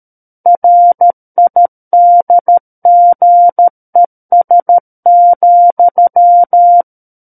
Morse code, heard as a keyed tone, about 13 wpm, 700 Hz. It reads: RIDGES,
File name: Book One / 04-Chapter 4 - 13 WPM